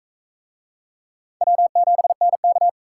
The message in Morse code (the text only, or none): W7NK